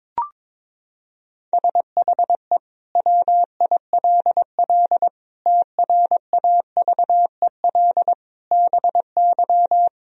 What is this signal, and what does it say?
Morse code, heard as a keyed tone, about 22 wpm, 700 Hz. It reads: SHE WILL TRAVEL BY
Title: Sentences from Top 1000 Words - Review Part 3 - 22wpm